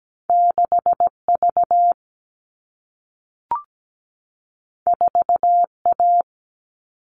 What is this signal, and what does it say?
Morse code, heard as a keyed tone, about 17 wpm, 700 Hz. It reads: 6V 4A
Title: All Cut Numbers - 2-Digit 17wpm